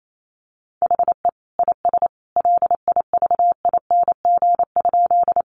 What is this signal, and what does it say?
Morse code, heard as a keyed tone, about 28 wpm, 700 Hz. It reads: HE IS LIVING?
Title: Sentences from Top 1100 Words 28wpm